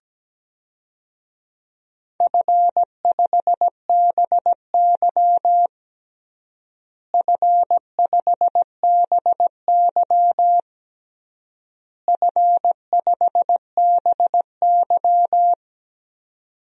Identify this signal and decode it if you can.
Morse code, heard as a keyed tone, about 17 wpm, 700 Hz. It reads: F5BY F5BY F5BY
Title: DXCC-100 Call Signs - 4 Characters - Repeated 3x - 17wpm